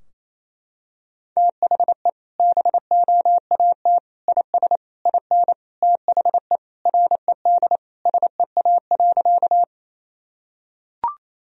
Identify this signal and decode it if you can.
Morse code, heard as a keyed tone, about 28 wpm, 700 Hz. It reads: THE BOAT IS IN THE RED SEA.
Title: Sentences from Top 1000 Words - Review Part 2 - 28wpm